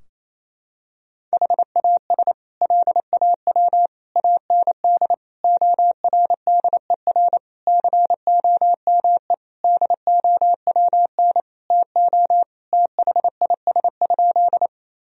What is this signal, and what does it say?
Morse code, heard as a keyed tone, about 27 wpm, 700 Hz. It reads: HAS LAW AND ORDER COME DOWN TO THIS?
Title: Sentences from Top 1000 Words - Review Part 3 - 28wpm